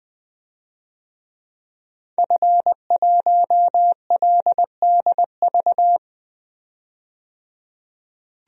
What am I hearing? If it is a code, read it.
Morse code, heard as a keyed tone, about 20 wpm, 700 Hz. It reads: F1LDV
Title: DXCC-100 Call Signs - 5 Characters - 20wpm